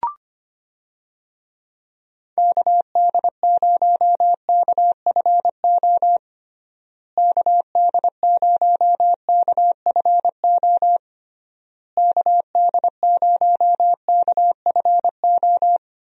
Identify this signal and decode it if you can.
Morse code, heard as a keyed tone, about 25 wpm, 700 Hz. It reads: KD0KFO KD0KFO KD0KFO
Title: USA Call Signs - 6 Characters - Repeated 3x - 25wpm